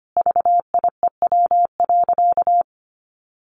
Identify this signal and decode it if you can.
Morse code, heard as a keyed tone, about 25 wpm, 700 Hz. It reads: VIEW.
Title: Sentences from Top 1100 Words 25wpm